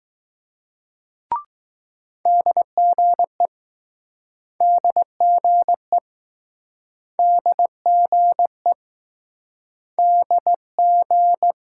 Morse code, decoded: DGE DGE DGE DG

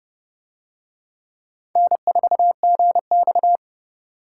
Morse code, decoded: N4GX

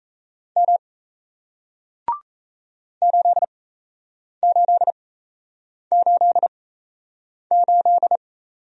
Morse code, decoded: M 8888

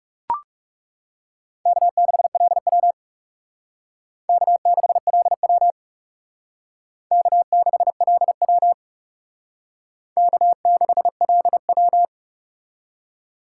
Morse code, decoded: K6LW K6LW K6LW K6LW

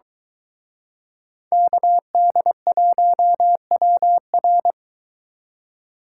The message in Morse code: KD1WR

